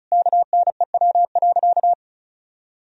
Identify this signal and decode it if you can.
Morse code, heard as a keyed tone, about 36 wpm, 700 Hz. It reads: KNEW.
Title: Sentences from Top 400 Words and Abbreviations 35wpm